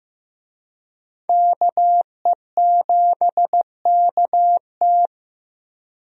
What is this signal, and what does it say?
Morse code, heard as a keyed tone, about 15 wpm, 700 Hz. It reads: KE7KT
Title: USA Call Signs - 5 Characters - Repeated 3x - 15wpm